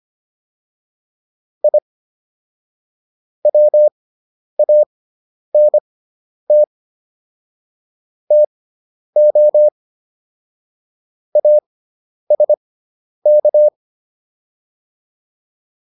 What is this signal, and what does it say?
Morse code, heard as a keyed tone, about 25 wpm, spaced as for 10 wpm, 600 Hz. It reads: I WANT TO ASK
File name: WD401 10